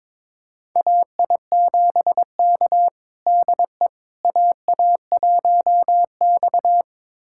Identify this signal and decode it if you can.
Morse code, heard as a keyed tone, about 22 wpm, 700 Hz. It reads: AI7K DE AA1X
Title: USA Callsign DE Callsign - 4 Characters - 22wpm